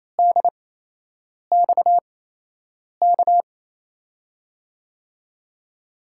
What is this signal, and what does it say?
Morse code, heard as a keyed tone, about 28 wpm, spaced as for 8 wpm, 700 Hz. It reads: DXK
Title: USA Call Signs - 5 Characters - 8x Character Spacing - 28wpm